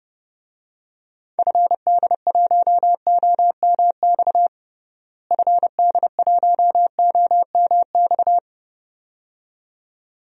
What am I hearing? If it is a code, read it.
Morse code, heard as a keyed tone, about 31 wpm, 700 Hz. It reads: FD1OMX FD1OMX
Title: DXCC-100 Call Signs - 6 Characters - Repeated 2x - 30wpm